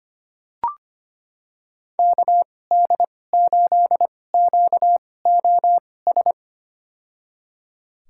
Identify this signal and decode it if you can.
Morse code, heard as a keyed tone, about 25 wpm, spaced as for 18 wpm, 700 Hz. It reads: KD8QOS
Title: USA Call Signs - 6 Characters - 2x Character Spacing - 25wpm